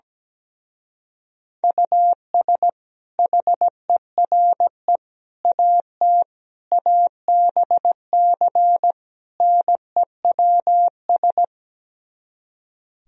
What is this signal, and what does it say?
Morse code, heard as a keyed tone, about 17 wpm, 700 Hz. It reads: US HERE AT ABC NEWS
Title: Speed-Racing - Sets of 5 Words 17wpm